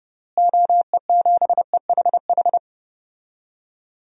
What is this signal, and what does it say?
Morse code, heard as a keyed tone, about 32 wpm, 700 Hz. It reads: OE7EHH
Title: DXCC-100 Call Signs - 6 Characters - Repeated 3x - 30wpm